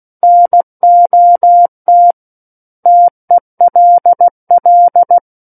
Morse code, heard as a keyed tone, about 16 wpm, 700 Hz. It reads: NOT TELL
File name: Book One / 07-Chapter 7 - 16 WPM